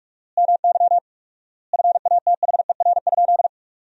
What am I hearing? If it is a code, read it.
Morse code, heard as a keyed tone, about 46 wpm, 700 Hz. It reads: MY FATHER?
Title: 4x Word-Spacing - Sentences from Top 400 Words - 45wpm